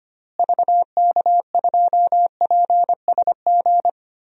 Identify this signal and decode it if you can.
Morse code, heard as a keyed tone, about 25 wpm, 700 Hz. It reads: VK2PSG
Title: DXCC-100 Call Signs - 6 Characters - Repeated 2x - 25wpm